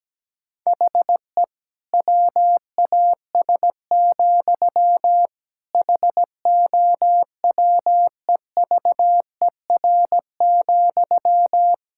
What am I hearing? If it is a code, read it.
Morse code, heard as a keyed tone, about 17 wpm, 700 Hz. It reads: HE WAS, HOWEVER,
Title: Sentences from Top 1000 Words - Review Part 5 - 17wpm